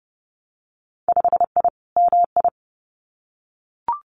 Morse code, heard as a keyed tone, about 30 wpm, 700 Hz. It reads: 5I MI
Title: Winter Field Day Exchanges Encore 30wpm